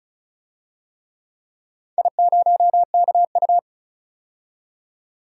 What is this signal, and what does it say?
Morse code, heard as a keyed tone, about 34 wpm, 700 Hz. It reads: I0KU
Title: DXCC-100 Call Signs - 4 Characters - 35wpm